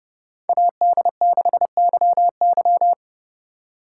Morse code, heard as a keyed tone, about 35 wpm, 700 Hz. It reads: AD6YY
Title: Speed Racing - USA 2x2 Call Signs 30wpm